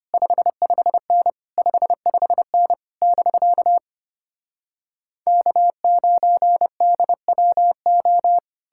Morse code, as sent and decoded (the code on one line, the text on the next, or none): ..... ..... -. / ..... ..... -. / -...-.- / -.- ----. -.. .-- ---
55N 55N <BK> K9DWO